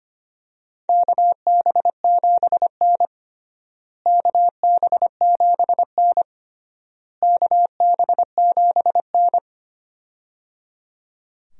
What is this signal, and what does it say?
Morse code, heard as a keyed tone, about 25 wpm, 700 Hz. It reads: KB7N KB7N KB7N
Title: USA Call Signs - 4 Characters - Repeated 3x - 25wpm